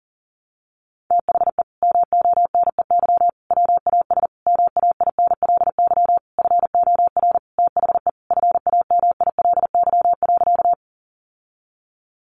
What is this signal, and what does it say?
Morse code, heard as a keyed tone, about 37 wpm, 700 Hz. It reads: THE MONEY WAS MAINLY FOR THE FAMILY.
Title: Sentences from Top 1400 Words 40wpm